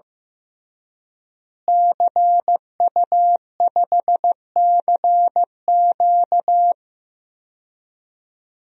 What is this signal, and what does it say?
Morse code, heard as a keyed tone, about 16 wpm, 700 Hz. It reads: CU5CQ
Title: DXCC-100 Call Signs - 5 Characters - Repeated 3x - 15wpm